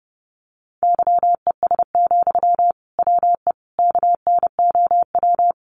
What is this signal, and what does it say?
Morse code, heard as a keyed tone, about 30 wpm, 700 Hz. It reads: YES, WE KNOW
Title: Sentences from Top 500 Words 30wpm